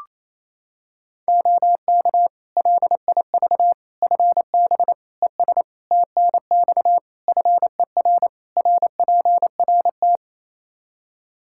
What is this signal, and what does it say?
Morse code, heard as a keyed tone, about 28 wpm, 700 Hz. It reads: OK LIV FB ES TNX FER RPRT
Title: Practice QSO - Exchange 2 - Info - 28wpm